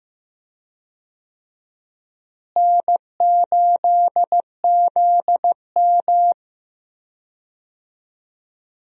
Morse code, heard as a keyed tone, about 16 wpm, 700 Hz. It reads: N8ZM
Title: USA Call Signs - 4 Characters - Repeated 3x - 15wpm